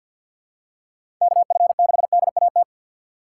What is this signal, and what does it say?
Morse code, heard as a keyed tone, about 52 wpm, 700 Hz. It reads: KF6DAT
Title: USA Call Signs - 6 Characters - 3x Character Spacing - 50wpm